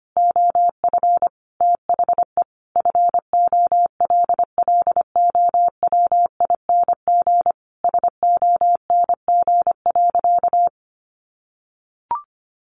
Morse code, heard as a keyed tone, about 25 wpm, 700 Hz. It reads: OF THE FOLLOWING SONG.
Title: Sentences from Top 900 Words 25wpm